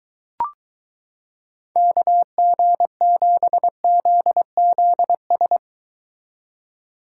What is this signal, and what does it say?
Morse code, heard as a keyed tone, about 23 wpm, 700 Hz. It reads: KG7ZZS